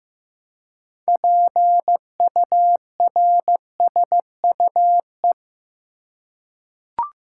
Morse code, heard as a keyed tone, about 15 wpm, 700 Hz. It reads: PURSUE
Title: Six Letter Words - 15wpm